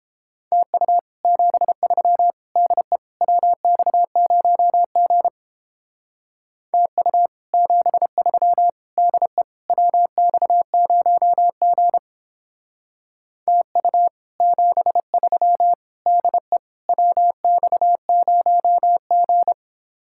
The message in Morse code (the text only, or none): TU 73 DE WX0G TU 73 DE WX0G TU 73 DE WX0G